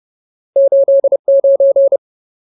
Morse code, heard as a keyed tone, about 30 wpm, 550 Hz. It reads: 89
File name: Ultimate Word List / 30WPM Ultimate